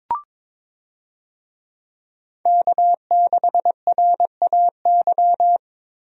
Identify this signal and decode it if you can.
Morse code, heard as a keyed tone, about 22 wpm, 700 Hz. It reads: K6RAY
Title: USA Call Signs - 5 Characters - Repeated 2x - 22wpm